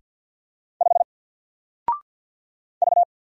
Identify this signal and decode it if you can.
Morse code, heard as a keyed tone, about 48 wpm, 700 Hz. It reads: 5 V
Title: Single Letter-Number Alt 50wpm